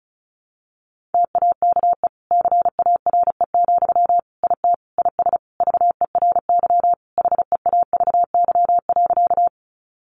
Morse code, heard as a keyed tone, about 34 wpm, 700 Hz. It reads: TAKE CARE, IT IS VERY HEAVY.
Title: Sentences from Top 1100 Words 35wpm